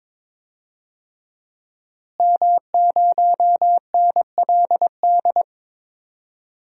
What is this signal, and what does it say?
Morse code, heard as a keyed tone, about 21 wpm, 700 Hz. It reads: M0NLD
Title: DXCC-100 Call Signs - 5 Characters - Repeated 2x - 22wpm